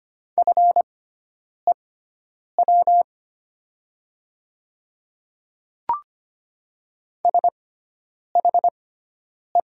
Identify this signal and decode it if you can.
Morse code, heard as a keyed tone, about 25 wpm, spaced as for 9 wpm, 700 Hz. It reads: FEW SHE
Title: 6x Character Spacing - Top 300 Words - 25wpm